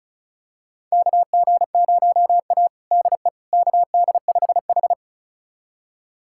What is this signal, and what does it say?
Morse code, heard as a keyed tone, about 34 wpm, 700 Hz. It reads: KG0A DE KD5H
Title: USA Callsign DE Callsign - 4 Characters - 35wpm